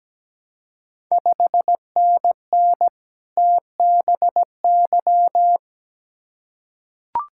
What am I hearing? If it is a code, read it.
Morse code, heard as a keyed tone, about 17 wpm, 700 Hz. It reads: 5NN TBY
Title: Ontario QSO Party Exchange 17wpm